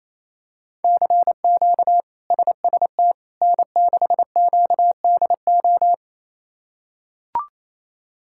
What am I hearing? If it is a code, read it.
Morse code, heard as a keyed tone, about 28 wpm, 700 Hz. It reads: CQ SST N6QDO
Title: K1USN SST QSO Practice - 28wpm